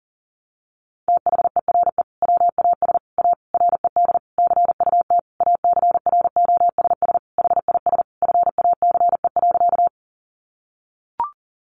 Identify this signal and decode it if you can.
Morse code, heard as a keyed tone, about 40 wpm, 700 Hz. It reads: THERE WAS A RED CUT ACROSS HIS FACE.
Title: Sentences from Top 700 Words 40wpm